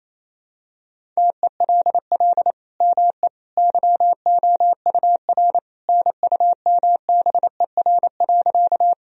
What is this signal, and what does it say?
Morse code, heard as a keyed tone, about 28 wpm, 700 Hz. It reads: TELL ME YOUR NUMBER.
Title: Sentences from Top 1000 Words - Review Part 1 - 28wpm